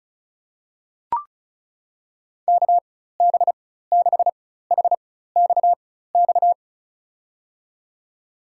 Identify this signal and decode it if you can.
Morse code, heard as a keyed tone, about 34 wpm, spaced as for 16 wpm, 700 Hz. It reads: KB6HXX
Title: USA Call Signs - 6 Characters - 4x Character Spacing - 35wpm